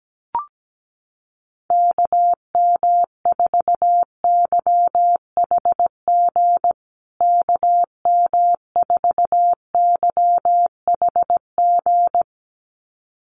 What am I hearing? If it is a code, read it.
Morse code, heard as a keyed tone, about 17 wpm, 700 Hz. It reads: KM4YHG KM4YHG
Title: USA Call Signs Repeated 2x 17wpm